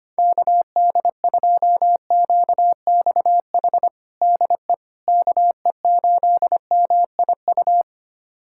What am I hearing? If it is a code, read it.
Morse code, heard as a keyed tone, about 25 wpm, 700 Hz. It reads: KD2QXH DE KE8MIU
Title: USA Callsign DE Callsign - 6 Characters - 25wpm